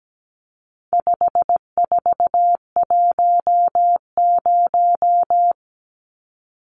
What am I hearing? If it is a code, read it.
Morse code, heard as a keyed tone, about 17 wpm, 700 Hz. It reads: 5410
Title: Numbers - 4-Digit 17wpm